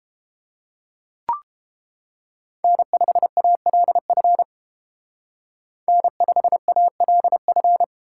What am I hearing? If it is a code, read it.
Morse code, heard as a keyed tone, about 28 wpm, 700 Hz. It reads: N5ALF N5ALF